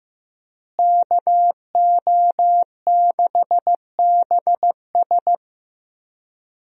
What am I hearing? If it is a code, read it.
Morse code, heard as a keyed tone, about 15 wpm, 700 Hz. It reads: KO6BS
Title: USA Call Signs - 5 Characters - Repeated 3x - 15wpm